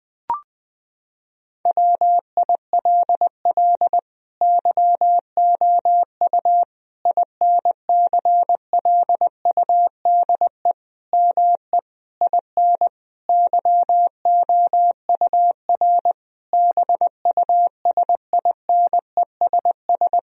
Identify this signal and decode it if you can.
Morse code, heard as a keyed tone, about 20 wpm, 700 Hz. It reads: WILL YOU INCLUDE ME IN YOUR BUSINESS
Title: Sentences from Top 1000 Words - Review Part 3 - 20wpm